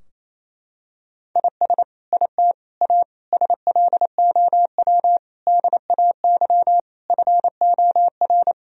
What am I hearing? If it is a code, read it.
Morse code, heard as a keyed tone, about 28 wpm, 700 Hz. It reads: IS IT A SLOW DAY FOR
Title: Sentences from Top 1000 Words - Review Part 2 - 28wpm